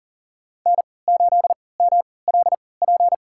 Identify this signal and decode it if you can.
Morse code, heard as a keyed tone, about 40 wpm, spaced as for 23 wpm, 700 Hz. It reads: N8MLP
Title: USA Call Signs - 5 Characters - 3x Character Spacing - 40wpm